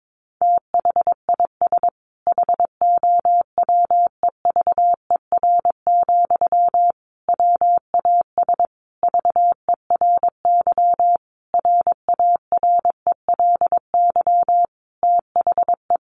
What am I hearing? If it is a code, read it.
Morse code, heard as a keyed tone, about 22 wpm, 700 Hz. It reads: THIS HOWEVER, WAS VERY RARELY THE
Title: Sentences from Top 1400 Words 22wpm